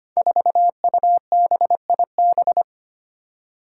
Morse code, heard as a keyed tone, about 25 wpm, 700 Hz. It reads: 4UBIB
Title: Five Challenging Letter-Number Combinations 25wpm